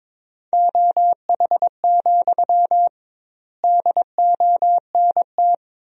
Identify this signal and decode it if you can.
Morse code, heard as a keyed tone, about 21 wpm, 700 Hz. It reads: OH, DONT
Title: 2x Word-Spacing - Sentences from Top 500 Words - 22wpm